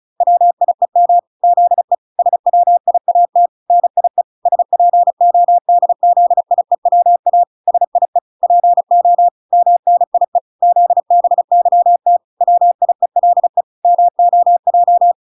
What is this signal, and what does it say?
Morse code, heard as a keyed tone, about 35 wpm, 700 Hz. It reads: WIEM ZE SWIAT NIE SPODZIEWA SIE PO MNIE ZBYT WIELE MOJ